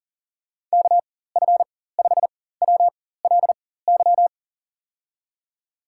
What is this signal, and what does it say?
Morse code, heard as a keyed tone, about 40 wpm, spaced as for 19 wpm, 700 Hz. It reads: KF5WLY
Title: USA Call Signs - 6 Characters - 4x Character Spacing - 40wpm